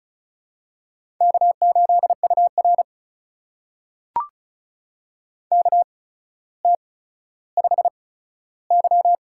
Morse code, heard as a keyed tone, about 34 wpm, spaced as for 15 wpm, 700 Hz. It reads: K8UR KT5Y
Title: USA Call Signs - 4 Characters - 8x Character Spacing - 35wpm